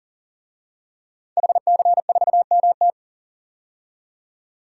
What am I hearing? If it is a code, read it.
Morse code, heard as a keyed tone, about 40 wpm, 700 Hz. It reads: HC4MT
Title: DXCC-100 Call Signs - 5 Characters - 40wpm